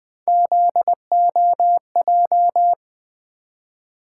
Morse code, heard as a keyed tone, about 20 wpm, 700 Hz. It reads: ZOJ
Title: Three Letter-Number Combinations - 20wpm